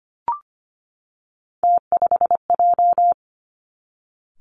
Morse code, heard as a keyed tone, about 25 wpm, 700 Hz. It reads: T5J
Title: Three Challenging Letter-Number Combinations 25wpm